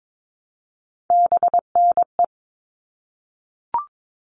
Morse code, heard as a keyed tone, about 22 wpm, 700 Hz. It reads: BNE
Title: All Cut Numbers - 3-Digit 22wpm